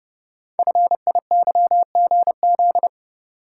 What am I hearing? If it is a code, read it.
Morse code, heard as a keyed tone, about 30 wpm, 700 Hz. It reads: FIYGZ
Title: Code Groups - 5 Letters - 30wpm